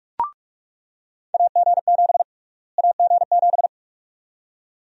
Morse code, heard as a keyed tone, about 42 wpm, 700 Hz. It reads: AG7 AG7
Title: USA Call Sign Prefixes - Repeated 2x - 45wpm